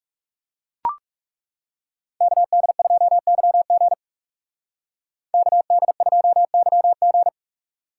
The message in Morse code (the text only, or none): KD2YG KD2YG